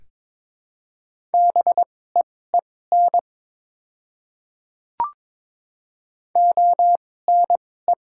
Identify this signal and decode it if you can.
Morse code, heard as a keyed tone, about 22 wpm, spaced as for 16 wpm, 700 Hz. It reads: BEEN ONE
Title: 2x Character Spacing - Top 100 Words - 22wpm